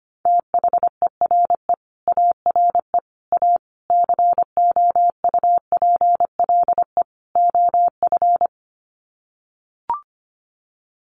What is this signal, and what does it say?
Morse code, heard as a keyed tone, about 25 wpm, 700 Hz. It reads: THERE ARE A COUPLE OF
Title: Sets of 5 Words Encore 25wpm